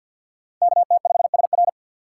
Morse code, heard as a keyed tone, about 53 wpm, 700 Hz. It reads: KT5SR
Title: USA Call Signs - 3x Character Spacing - 50wpm